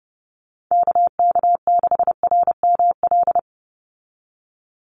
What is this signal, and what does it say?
Morse code, heard as a keyed tone, about 30 wpm, 700 Hz. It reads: KK6RML
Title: USA Call Signs 30wpm